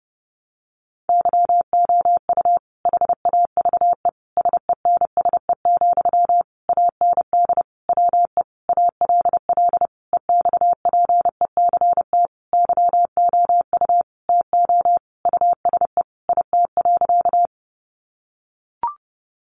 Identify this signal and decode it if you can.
Morse code, heard as a keyed tone, about 30 wpm, 700 Hz. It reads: YOU HAVE SENSE, AND WE ALL EXPECT YOU TO USE IT.
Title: Sentences from Top 1100 Words 30wpm